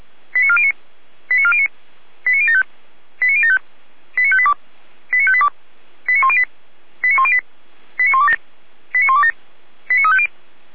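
Five-tone selective call (ZVEI), heard as quick runs of five tones, tones of about 70 ms. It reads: 89409 89409 89075 89075 89582 89582 89108 89108 89127 89127 89350
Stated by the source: Received on an SDR.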